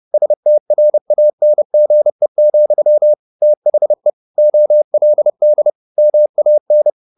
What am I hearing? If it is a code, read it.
Morse code, heard as a keyed tone, about 30 wpm, 600 Hz. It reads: STRANGE, THE OLD MAN
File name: the old man and the sea / the old man and the sea 1 30wpm